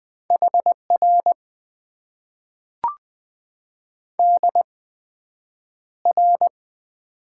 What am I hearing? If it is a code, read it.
Morse code, heard as a keyed tone, about 20 wpm, spaced as for 9 wpm, 700 Hz. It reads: HR DR
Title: Top 90 QSO Elements - 8x Character Spacing - 20wpm